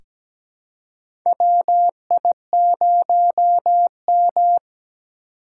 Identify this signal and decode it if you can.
Morse code, heard as a keyed tone, about 17 wpm, 700 Hz. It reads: WI0M
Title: USA Call Signs - 4 Characters - 3x Character Spacing - 17wpm